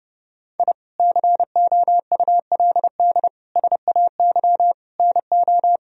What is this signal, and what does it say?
Morse code, heard as a keyed tone, about 30 wpm, 700 Hz. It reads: I COULD SAY NO
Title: Sentences from Top 1000 Words - Review Part 2 - 30wpm